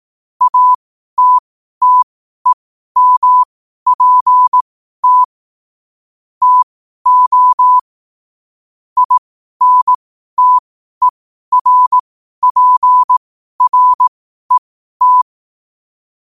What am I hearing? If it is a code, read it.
Morse code, heard as a keyed tone, about 18 wpm, spaced as for 12 wpm, 1000 Hz. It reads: ATTEMPT TO INTERPRET
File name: qotdfiles / 12wpm